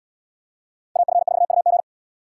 Morse code, heard as a keyed tone, about 77 wpm, 700 Hz. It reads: AF4UG